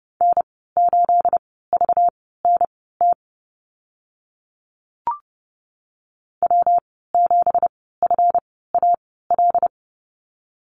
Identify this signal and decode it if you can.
Morse code, heard as a keyed tone, about 30 wpm, spaced as for 17 wpm, 700 Hz. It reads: N8VNT W7FAL
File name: Lesson 37g - Letter x - Call Signs 30wpm